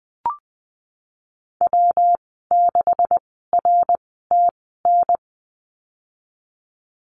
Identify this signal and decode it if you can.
Morse code, heard as a keyed tone, about 20 wpm, spaced as for 15 wpm, 700 Hz. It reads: W6RTN
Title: 128 Lesson 21g - Number 6 - Call Signs 20wpm